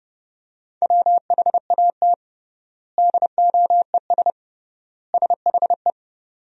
Morse code, heard as a keyed tone, about 30 wpm, 700 Hz. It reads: WHAT DOES SHE
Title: 3x Word-Spacing - Sentences from Top 200 Words - 30wpm